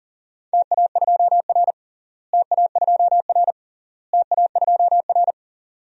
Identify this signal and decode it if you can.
Morse code, heard as a keyed tone, about 40 wpm, 700 Hz. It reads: TA2R TA2R TA2R
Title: DXCC-100 Call Signs - 4 Characters - Repeated 3x - 40wpm